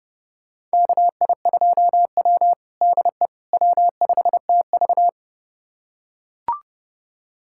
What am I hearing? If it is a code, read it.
Morse code, heard as a keyed tone, about 32 wpm, 700 Hz. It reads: KI2W DE W5TV
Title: USA Callsign DE Callsign - 4 Characters - 30wpm